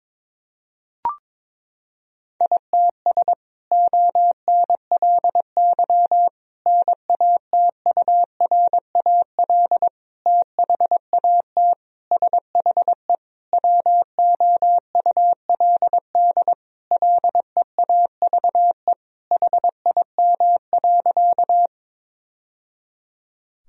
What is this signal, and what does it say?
Morse code, heard as a keyed tone, about 22 wpm, 700 Hz. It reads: ITS ONLY NATURAL THAT SHE WOULD LEAVE HIM.
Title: Sentences from Top 1000 Words - Review Part 4 - 22wpm